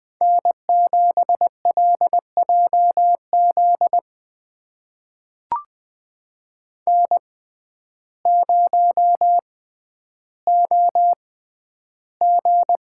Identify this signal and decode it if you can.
Morse code, heard as a keyed tone, about 20 wpm, spaced as for 11 wpm, 700 Hz. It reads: N7LJZ N0OG
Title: USA Call Signs - 5 Characters - 6x Character Spacing - 20wpm